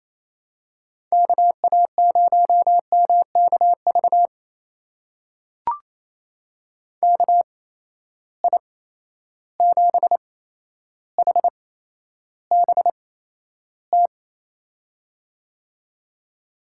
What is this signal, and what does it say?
Morse code, heard as a keyed tone, about 28 wpm, spaced as for 12 wpm, 700 Hz. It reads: KA0MKV KI7HBT
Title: USA Call Signs - 6 Characters - 8x Character Spacing - 28wpm